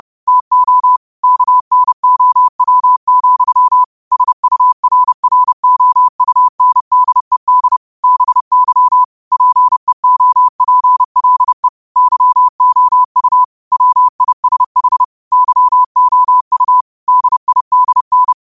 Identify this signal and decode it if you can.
Morse code, heard as a keyed tone, about 30 wpm, 1000 Hz. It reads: TO KNOW, SURROUNDED BY PEOPLE YOU WISH YOU DIDN
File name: qotdfiles / 30wpm